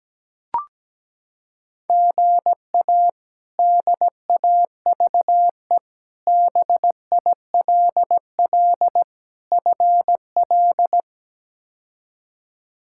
Morse code, heard as a keyed tone, about 17 wpm, 700 Hz. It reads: GA DAVE BILL FL
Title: K1USN SST QSO Practice - 17wpm